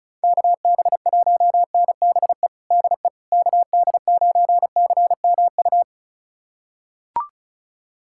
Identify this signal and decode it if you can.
Morse code, heard as a keyed tone, about 35 wpm, 700 Hz. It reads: KB1NBE DE KD9CMU
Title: USA Callsign DE Callsign - 6 Characters - 35wpm